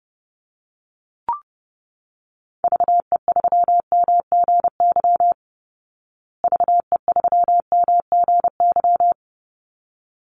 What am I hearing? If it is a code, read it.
Morse code, heard as a keyed tone, about 30 wpm, 700 Hz. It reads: VE3MGY VE3MGY
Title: Call Signs Repeated 3x - 30wpm